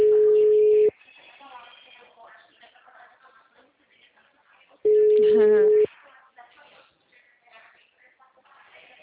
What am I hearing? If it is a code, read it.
Ringback tone (Europe, 425 Hz), with speech.